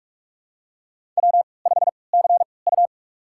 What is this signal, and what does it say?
Morse code, heard as a keyed tone, about 43 wpm, spaced as for 25 wpm, 700 Hz. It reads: W5CU
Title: USA Call Signs - 4 Characters - 3x Character Spacing - 45wpm